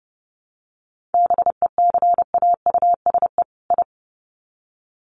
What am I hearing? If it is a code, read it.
Morse code, heard as a keyed tone, about 29 wpm, 700 Hz. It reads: BECAUSE I
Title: Sets of 2 Words 30wpm